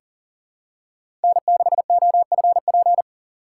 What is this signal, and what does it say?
Morse code, heard as a keyed tone, about 40 wpm, 700 Hz. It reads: N6OFP
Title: USA Call Signs - 5 Characters - 2x Character Spacing - 40wpm